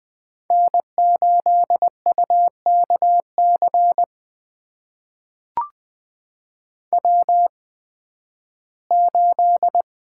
Morse code, heard as a keyed tone, about 20 wpm, spaced as for 13 wpm, 700 Hz. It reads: N8UKC W8
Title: USA Call Signs - 5 Characters - 8x Character Spacing - 20wpm